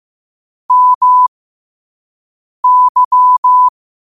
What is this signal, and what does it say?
Morse code, heard as a keyed tone, about 15 wpm, spaced as for 5 wpm, 1000 Hz. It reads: MY
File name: qotdfiles / 5wpm